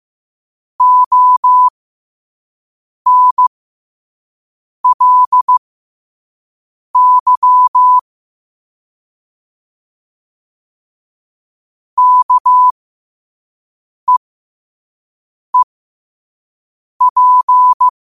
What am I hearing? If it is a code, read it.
Morse code, heard as a keyed tone, about 15 wpm, spaced as for 5 wpm, 1000 Hz. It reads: ONLY KEEP
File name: qotdfiles / 5wpm